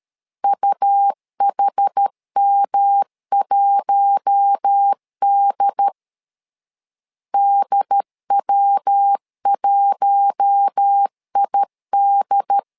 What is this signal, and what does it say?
Morse code, heard as a keyed tone, about 13 wpm, 800 Hz. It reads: UHM1D DW1ID
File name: LEZIONI / 4 Q1 12wpm 2028